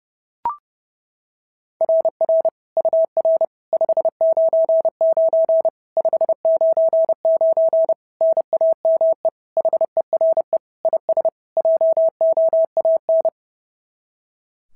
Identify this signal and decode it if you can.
Morse code, heard as a keyed tone, about 30 wpm, 690 Hz. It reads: RR UR 599 599 NAME HERE IS JOAN
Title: SOTA QSO Practice 30wpm